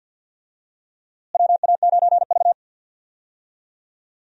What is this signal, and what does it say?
Morse code, heard as a keyed tone, about 48 wpm, 700 Hz. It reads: WA9V
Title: USA Call Signs - 4 Characters - Repeated 2x - 50wpm